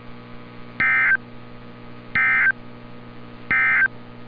Emergency Alert System (SAME) tones, heard as harsh buzzing bursts. It is heard from 0.7 to 4.0 s.